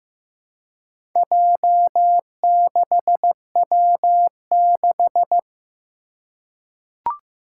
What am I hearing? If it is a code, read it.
Morse code, heard as a keyed tone, about 15 wpm, 700 Hz. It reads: J6W6
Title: Four Challenging Letter-Number Combinations 15wpm